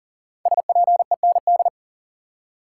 Morse code, heard as a keyed tone, about 38 wpm, 700 Hz. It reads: SPEND
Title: Five Letter Words - 40wpm